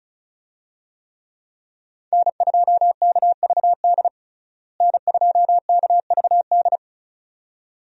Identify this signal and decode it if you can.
Morse code, heard as a keyed tone, about 36 wpm, 700 Hz. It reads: N2KVD N2KVD
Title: USA Call Signs - 5 Characters - Repeated 2x - 35wpm